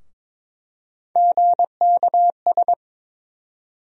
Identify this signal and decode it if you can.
Morse code, heard as a keyed tone, about 22 wpm, 700 Hz. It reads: GKS
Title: USA Call Sign Postfixes - 3 Letters 22wpm